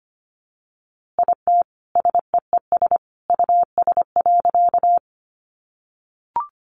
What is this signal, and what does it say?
Morse code, heard as a keyed tone, about 25 wpm, 700 Hz. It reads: IT SEES US.
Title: A Sound Of Thunder 25wpm